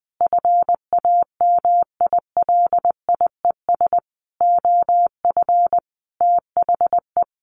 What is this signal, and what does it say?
Morse code, heard as a keyed tone, about 20 wpm, 700 Hz. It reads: FAMILIES OF THE
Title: A Sound Of Thunder 20wpm